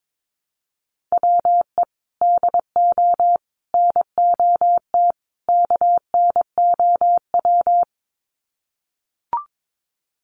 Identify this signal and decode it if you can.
Morse code, heard as a keyed tone, about 22 wpm, 700 Hz. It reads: WE DO NOT KNOW
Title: Sets of 4 Words 22wpm